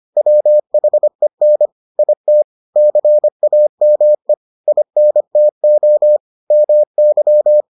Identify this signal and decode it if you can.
Morse code, heard as a keyed tone, about 25 wpm, 600 Hz. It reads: WHEN IT CAME INTO MY